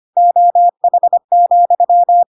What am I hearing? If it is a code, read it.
Morse code, heard as a keyed tone, about 25 wpm, 700 Hz. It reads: OH,
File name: Pinocchio / Pinocchio ch7 25 25 ik1hge